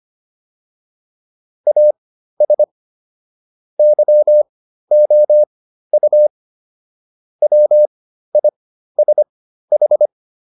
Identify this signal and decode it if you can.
Morse code, heard as a keyed tone, about 25 wpm, spaced as for 13 wpm, 600 Hz. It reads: AS YOU WISH